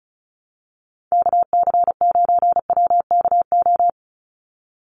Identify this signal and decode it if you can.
Morse code, heard as a keyed tone, about 34 wpm, 700 Hz. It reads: KC9WKO
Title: USA Call Signs 35wpm